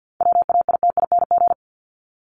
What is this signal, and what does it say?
Morse code, heard as a keyed tone, about 50 wpm, 700 Hz. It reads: WAITING
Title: ING Suffix - Top 100 Words with Base Words 50wpm